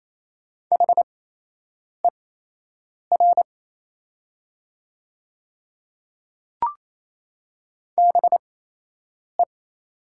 Morse code, heard as a keyed tone, about 28 wpm, spaced as for 8 wpm, 700 Hz. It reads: HER BE
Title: 8x Character Spacing - Top 100 Words - 28wpm